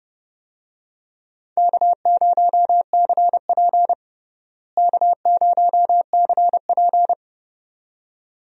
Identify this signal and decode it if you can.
Morse code, heard as a keyed tone, about 30 wpm, 700 Hz. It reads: K0CP K0CP
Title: USA Call Signs - 4 Characters - Repeated 2x - 30wpm